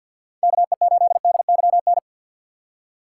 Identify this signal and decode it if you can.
Morse code, heard as a keyed tone, about 53 wpm, 700 Hz. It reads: KE8DYN
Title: USA Call Signs - 6 Characters - 3x Character Spacing - 50wpm